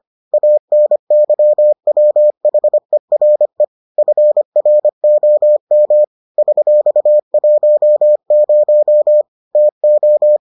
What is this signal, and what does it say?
Morse code, heard as a keyed tone, about 25 wpm, 600 Hz. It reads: ANYWHERE FROM $10 TO